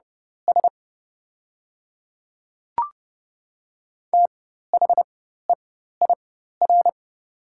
Morse code, heard as a keyed tone, about 30 wpm, spaced as for 14 wpm, 700 Hz. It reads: S THEIR
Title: 4x Character Spacing - Top 500 Words - 30wpm